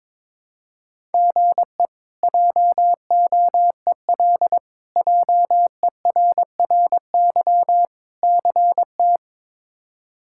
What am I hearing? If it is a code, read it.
Morse code, heard as a keyed tone, about 22 wpm, 700 Hz. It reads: GE JOEL JERRY CT
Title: K1USN SST QSO Practice - 22wpm